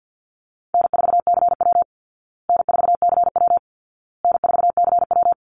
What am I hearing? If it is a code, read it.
Morse code, heard as a keyed tone, about 49 wpm, 700 Hz. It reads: N4CW N4CW N4CW
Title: Call Signs Repeated 2x - 50wpm